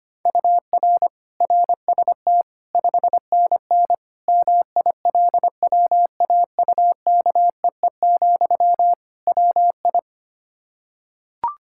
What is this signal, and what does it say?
Morse code, heard as a keyed tone, about 25 wpm, 700 Hz. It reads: UR RST 5NN MILWAUKEE, WI
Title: K1USN SST QSO Practice 25wpm